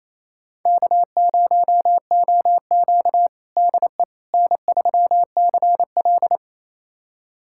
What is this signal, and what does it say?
Morse code, heard as a keyed tone, about 28 wpm, 700 Hz. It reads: K0OQ DE N3CL
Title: USA Callsign DE Callsign - 4 Characters - 28wpm